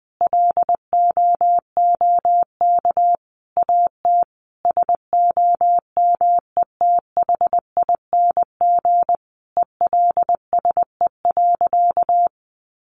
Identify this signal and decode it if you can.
Morse code, heard as a keyed tone, about 20 wpm, 700 Hz. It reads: LOOK AT SOMETHING ELSE.
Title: Sentences from Top 800 Words 20wpm